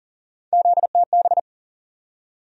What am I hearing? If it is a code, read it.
Morse code, heard as a keyed tone, about 40 wpm, 700 Hz. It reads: ZTB
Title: USA Call Sign Postfixes - 3 Letters 40wpm